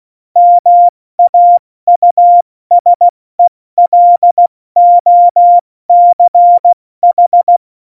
Morse code, heard as a keyed tone, about 16 wpm, 700 Hz. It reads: MAUSELOCH
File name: Morsekurs / Witze / Witz 18. September 2018, 70 F80